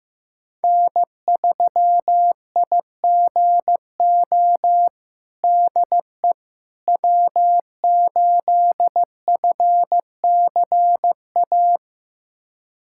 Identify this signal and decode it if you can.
Morse code, heard as a keyed tone, about 15 wpm, 700 Hz. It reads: N3IGO DE W8FCA
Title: USA Callsign DE Callsign - 5 Characters - 15wpm